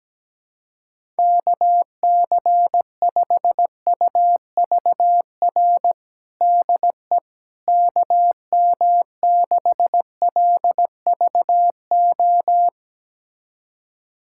Morse code, heard as a keyed tone, about 17 wpm, 700 Hz. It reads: KC5UVR DE KM6LVO
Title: USA Callsign DE Callsign 17wpm